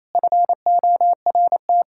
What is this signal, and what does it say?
Morse code, heard as a keyed tone, about 28 wpm, 700 Hz. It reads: FORT
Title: Practice QSO - Exchange 1 - Beginning - 28wpm